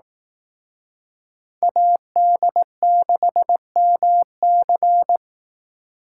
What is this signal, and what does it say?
Morse code, heard as a keyed tone, about 18 wpm, 700 Hz. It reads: AD6MC
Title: Speed Racing - USA Call Signs 15wpm